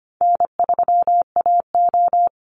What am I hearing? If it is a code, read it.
Morse code, heard as a keyed tone, about 25 wpm, 700 Hz. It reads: N3AO
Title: Easy Call Signs Repeated 2x - 25wpm